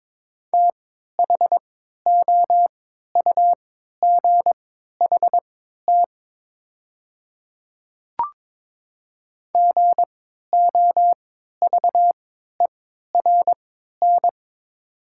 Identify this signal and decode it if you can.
Morse code, heard as a keyed tone, about 22 wpm, spaced as for 12 wpm, 700 Hz. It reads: THOUGHT GOVERN
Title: 3x Character Spacing - Top 500 Words - 22wpm